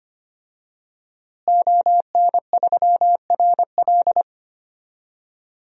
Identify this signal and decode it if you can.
Morse code, heard as a keyed tone, about 25 wpm, 700 Hz. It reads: ON3RL
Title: DXCC-100 Call Signs - 5 Characters - Repeated 3x - 25wpm